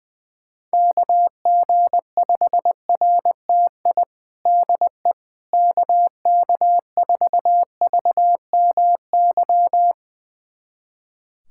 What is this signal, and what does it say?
Morse code, heard as a keyed tone, about 20 wpm, 700 Hz. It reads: KG5RTI DE KK4VMY
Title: USA Callsign DE Callsign 20wpm